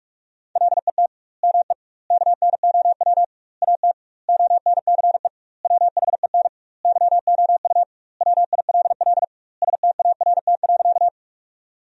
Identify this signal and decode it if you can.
Morse code, heard as a keyed tone, about 46 wpm, 700 Hz. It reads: LET ME KNOW AT ONCE WHEN YOU WILL START.
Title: 2x Word-Spacing - Sentences from Top 300 Words - 45wpm